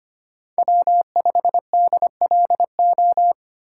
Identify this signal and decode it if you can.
Morse code, heard as a keyed tone, about 25 wpm, 700 Hz. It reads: W5DLO
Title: USA Call Signs - 5 Characters - 4x Character Spacing - 25wpm